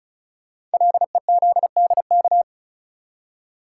Morse code, heard as a keyed tone, about 36 wpm, 690 Hz. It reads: LEZDK
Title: Code Groups - 5 Letters - 35wpm